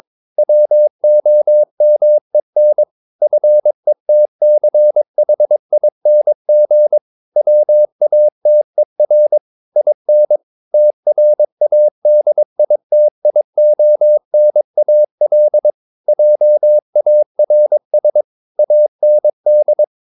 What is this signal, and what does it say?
Morse code, heard as a keyed tone, about 22 wpm, 600 Hz. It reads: WOMEN FETCHING WATER IN TRADITIONAL JARS AND
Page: Romanen Never af Ken Follet på engelsk.